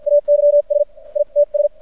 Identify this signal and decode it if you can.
Morse code, heard as a keyed tone, about 53 wpm, spaced as for 43 wpm, 580 Hz. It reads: AYS ITS
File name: archived / CRSnet